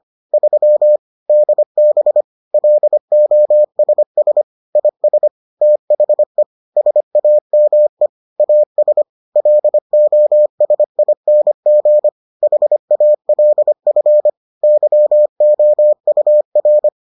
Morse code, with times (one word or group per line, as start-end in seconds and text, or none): 0.3-1.0 3
1.3-2.2 DB
2.5-4.4 LOSS
4.8-5.3 IS
5.6-6.4 THE
6.8-8.1 SAME
8.4-9.0 AS
9.4-12.1 LOSING
12.4-14.3 HALF
14.6-16.9 YOUR